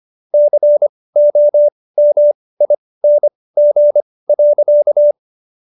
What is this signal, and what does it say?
Morse code, heard as a keyed tone, about 25 wpm, spaced as for 18 wpm, 590 Hz. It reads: COMING.